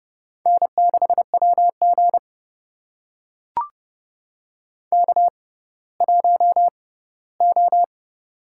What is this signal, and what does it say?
Morse code, heard as a keyed tone, about 30 wpm, spaced as for 17 wpm, 700 Hz. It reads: N6WG K1O
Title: USA Call Signs - 4 Characters - 6x Character Spacing - 30wpm